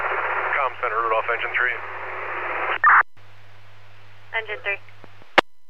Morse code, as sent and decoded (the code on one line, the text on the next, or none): none